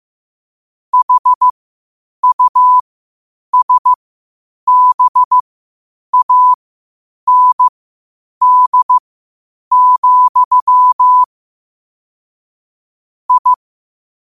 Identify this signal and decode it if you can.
Morse code, heard as a keyed tone, about 15 wpm, spaced as for 8 wpm, 1000 Hz. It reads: HUSBAND, I
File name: qotdfiles / 8wpm